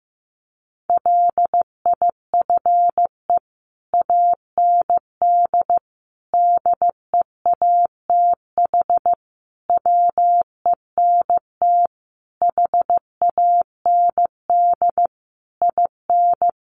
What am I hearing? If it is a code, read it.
Morse code, heard as a keyed tone, about 14 wpm, 700 Hz. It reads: LIFE AND DEATH WENT HAND IN
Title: Sentences from Top 1100 Words 15wpm